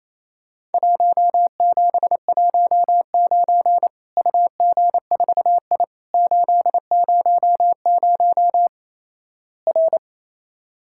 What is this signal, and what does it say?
Morse code, heard as a keyed tone, about 28 wpm, 700 Hz. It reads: 1719 UG4I 800 R
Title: Worked All Europe DX Contest - QTC Exchange - 28wpm